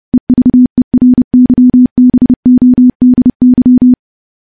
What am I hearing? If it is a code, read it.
Morse code, heard as a keyed tone, about 30 wpm, 250 Hz. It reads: EVERYBODY
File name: morse-training-30WPM